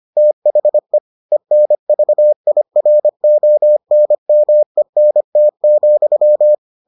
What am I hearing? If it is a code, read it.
Morse code, heard as a keyed tone, about 25 wpm, 600 Hz. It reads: THE ENVIRONMENT,